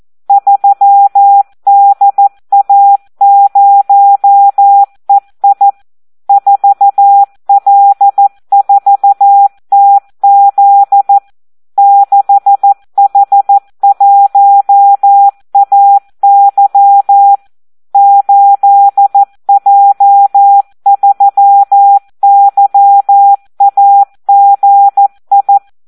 Morse code, heard as a keyed tone, about 14 wpm, 800 Hz. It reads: 3DA0EI 4L4TZ 6H1AY 8J3YAGI
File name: Audio practica CW / SD 1 14